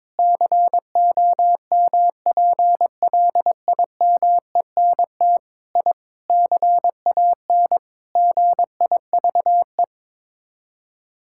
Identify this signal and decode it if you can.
Morse code, heard as a keyed tone, about 22 wpm, 700 Hz. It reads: COMPLIMENT I CAN GIVE
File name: practice / 22wpm / Sets of 8 Words - 22wpm